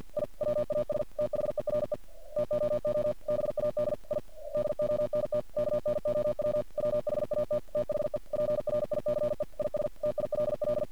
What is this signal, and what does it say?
Morse code, heard as a keyed tone, about 51 wpm, 600 Hz. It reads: I WAS THERE TOO BAD I DONT KNOW WHAT THE PRICE IS TILL
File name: archived / CRSnet